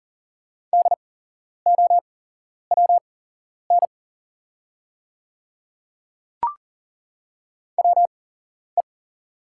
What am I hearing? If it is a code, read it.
Morse code, heard as a keyed tone, about 40 wpm, spaced as for 11 wpm, 700 Hz. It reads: DOWN WE
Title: 8x Character Spacing - Top 100 Words - 40wpm